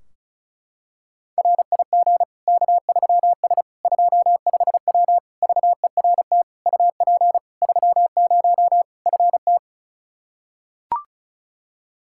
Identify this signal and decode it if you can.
Morse code, heard as a keyed tone, about 34 wpm, 700 Hz. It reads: RIG K3S 25W VERT UP 30 FT
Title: Practice QSO - Exchange 2 - Info - 35wpm